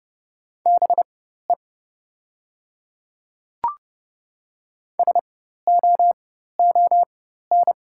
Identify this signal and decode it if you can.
Morse code, heard as a keyed tone, about 30 wpm, spaced as for 14 wpm, 700 Hz. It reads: BE SOON
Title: 4x Character Spacing - Top 500 Words - 30wpm